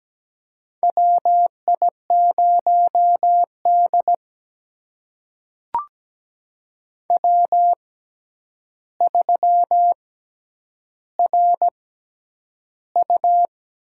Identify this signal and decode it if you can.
Morse code, heard as a keyed tone, about 17 wpm, spaced as for 9 wpm, 700 Hz. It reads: WI0D W3RU
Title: USA Call Signs - 4 Characters - 6x Character Spacing - 17wpm